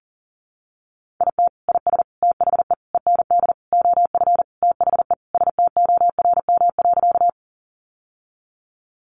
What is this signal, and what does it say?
Morse code, heard as a keyed tone, about 40 wpm, 700 Hz. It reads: IT IS THE END OF THE STORM.
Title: Sentences from Top 1500 Words 40wpm